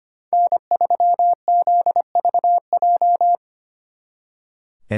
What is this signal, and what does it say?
Morse code, heard as a keyed tone, about 29 wpm, 700 Hz. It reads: N3ZVJ
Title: Speed Racing - USA 1x3 Call Signs 25wpm